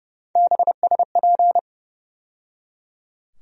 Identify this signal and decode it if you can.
Morse code, heard as a keyed tone, about 32 wpm, 700 Hz. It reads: BSP
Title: USA Call Sign Postfixes - 3 Letters 30wpm